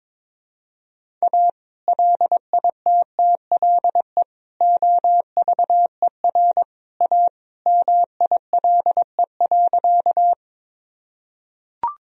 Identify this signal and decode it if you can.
Morse code, heard as a keyed tone, about 22 wpm, 700 Hz. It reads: A LITTLE OVER A MILE.
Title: Sentences from Top 1000 Words - Review Part 1 - 22wpm